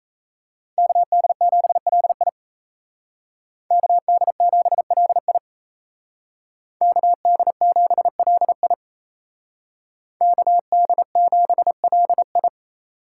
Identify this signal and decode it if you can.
Morse code, heard as a keyed tone, about 36 wpm, 700 Hz. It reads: KD7LI KD7LI KD7LI KD7LI